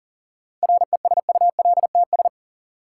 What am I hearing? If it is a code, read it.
Morse code, heard as a keyed tone, about 40 wpm, 700 Hz. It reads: RESULTS
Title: Sets of 4 Words - New Word Drill - 40wpm